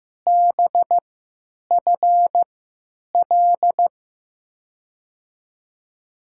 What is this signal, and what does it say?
Morse code, heard as a keyed tone, about 15 wpm, spaced as for 8 wpm, 700 Hz. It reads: BFL
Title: USA Call Signs - 6 Characters - 3x Character Spacing - 15wpm